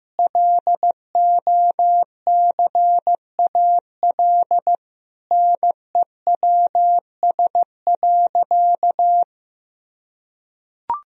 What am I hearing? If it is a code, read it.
Morse code, heard as a keyed tone, about 15 wpm, 700 Hz. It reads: LOCAL NEWS.
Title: Sentences from Top 1000 Words - Review Part 3 - 15wpm